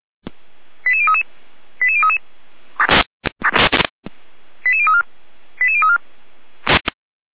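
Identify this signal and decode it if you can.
Five-tone selective call (ZVEI), heard as quick runs of five tones, tones of about 70 ms. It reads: 88033 88033 88034 88034
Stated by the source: ZVEI five-tone call 88139 received by SDR, archive path 455.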